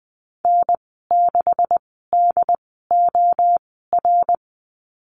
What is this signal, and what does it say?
Morse code, heard as a keyed tone, about 20 wpm, spaced as for 15 wpm, 700 Hz. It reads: N6DOR
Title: 140 Lesson 23g - Letter f - Call Signs 20wpm